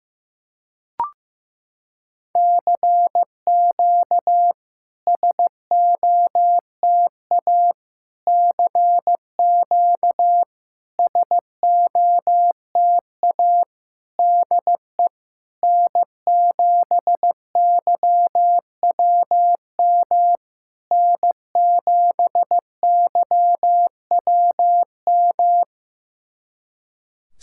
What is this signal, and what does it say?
Morse code, heard as a keyed tone, about 15 wpm, 700 Hz. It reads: CQ SOTA CQ SOTA DE N7YWM N7YWM
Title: SOTA QSO Practice 15wpm